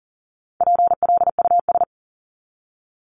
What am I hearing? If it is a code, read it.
Morse code, heard as a keyed tone, about 40 wpm, 700 Hz. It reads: PLUS
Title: Top 1001-1250 Words 40wpm